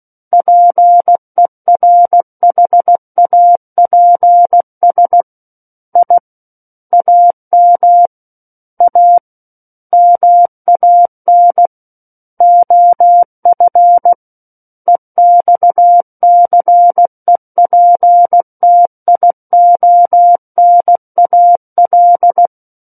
Morse code, heard as a keyed tone, about 16 wpm, 700 Hz. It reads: PERHAPS I AM A MAN OF EXCEPTIONAL
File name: Book One / 07-Chapter 7 - 16 WPM